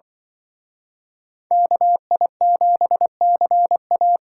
Morse code, heard as a keyed tone, about 25 wpm, 700 Hz. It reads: KI7CA
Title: Speed Racing - USA Call Signs 20wpm